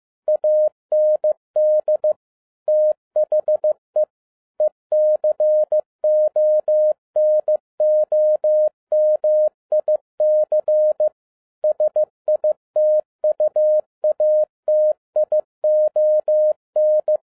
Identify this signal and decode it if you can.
Morse code, heard as a keyed tone, about 15 wpm, 610 Hz. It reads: AND THE ECONOMIC SITUATION
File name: 2011-07-mp3 / n07 A LOT HAS CHANGED 15wpm